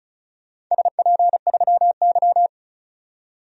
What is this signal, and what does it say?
Morse code, heard as a keyed tone, about 34 wpm, 700 Hz. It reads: SP3Y
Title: DXCC-100 Call Signs - 4 Characters - 35wpm